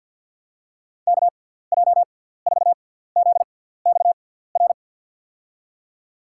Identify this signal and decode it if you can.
Morse code, heard as a keyed tone, about 54 wpm, spaced as for 17 wpm, 700 Hz. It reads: KJ4ZXR
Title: USA Call Signs - 6 Characters - 6x Character Spacing - 50wpm